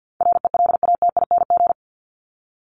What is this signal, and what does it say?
Morse code, heard as a keyed tone, about 50 wpm, 700 Hz. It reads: RELATING
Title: ING Suffix - Top 100 Words with Base Words 50wpm